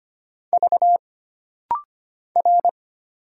Morse code, heard as a keyed tone, about 25 wpm, 700 Hz. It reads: V R
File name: Single Character (Letter:Number:Punctuation:Prosign) - Rapid-Fire - 25wpm